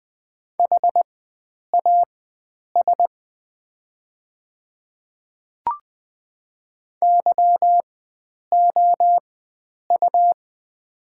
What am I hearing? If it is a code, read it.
Morse code, heard as a keyed tone, about 20 wpm, spaced as for 9 wpm, 700 Hz. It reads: HAS YOU
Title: 4x Character Spacing - Top 100 Words - 20wpm